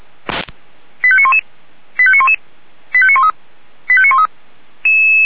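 Five-tone selective call (ZVEI), heard as quick runs of five tones, tones of about 70 ms. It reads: 85911 85911 85913 85913
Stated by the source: ZVEI five-tone selective call received on an SDR.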